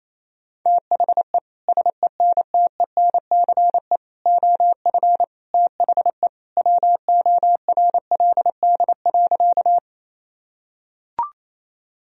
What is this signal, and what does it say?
Morse code, heard as a keyed tone, about 28 wpm, 700 Hz. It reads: THE SENTENCE OF THE WORLD.
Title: Sentences from Top 1000 Words - Review Part 1 - 28wpm